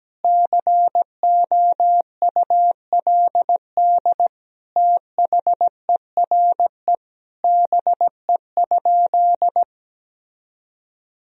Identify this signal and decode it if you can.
Morse code, heard as a keyed tone, about 17 wpm, 700 Hz. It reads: COULD THERE BE?
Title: Sentences from Top 1000 Words - Review Part 3 - 17wpm